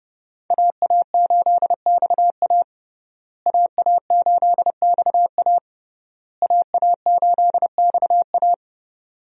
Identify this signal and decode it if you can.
Morse code, heard as a keyed tone, about 30 wpm, 700 Hz. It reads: AA8XA AA8XA AA8XA
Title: USA Call Signs - 5 Characters - Repeated 3x - 30wpm